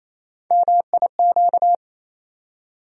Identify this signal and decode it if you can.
Morse code, heard as a keyed tone, about 28 wpm, 700 Hz. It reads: MIQ
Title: USA Call Sign Postfixes - 3 Letters 28wpm